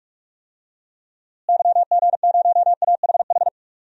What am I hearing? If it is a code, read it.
Morse code, heard as a keyed tone, about 47 wpm, 700 Hz. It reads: YG0AHH
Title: DXCC-100 Call Signs - 6 Characters - Repeated 3x - 45wpm